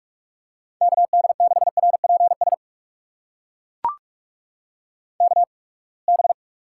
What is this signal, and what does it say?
Morse code, heard as a keyed tone, about 41 wpm, spaced as for 30 wpm, 700 Hz. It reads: KD6RPS KB
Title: USA Call Signs - 6 Characters - 8x Character Spacing - 45wpm